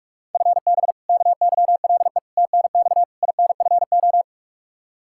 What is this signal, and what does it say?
Morse code, heard as a keyed tone, about 44 wpm, 700 Hz. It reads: FB KYLE TNX INFO
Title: Practice QSO - Exchange 3 - More Info - 45wpm